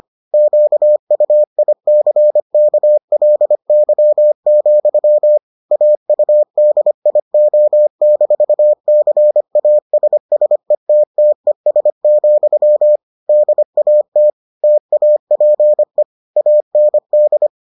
Morse code, heard as a keyed tone, about 25 wpm, 600 Hz. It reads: QUICKLY, AUDIO-CASSETTES, DAT TAPE AND